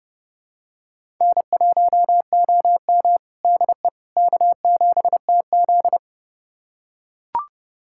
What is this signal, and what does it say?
Morse code, heard as a keyed tone, about 32 wpm, 700 Hz. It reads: N1OM DE K7TZ
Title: USA Callsign DE Callsign - 4 Characters - 30wpm